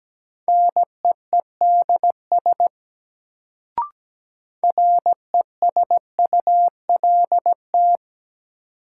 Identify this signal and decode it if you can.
Morse code, heard as a keyed tone, about 17 wpm, 700 Hz. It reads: NEEDS RESULT
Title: Sets of 3 Words - New Word Drill - 17wpm